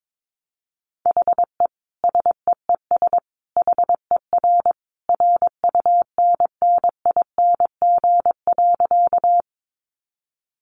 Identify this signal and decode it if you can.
Morse code, heard as a keyed tone, about 22 wpm, 700 Hz. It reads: HE SEES HER RUNNING.
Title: Additional Easy Phrases 22wpm